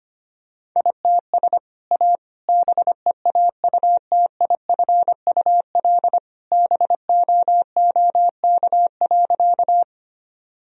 Morse code, heard as a keyed tone, about 25 wpm, 700 Hz. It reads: ITS A BEAUTIFUL BOOK.
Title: Sentences from Top 1000 Words - Review Part 1 - 25wpm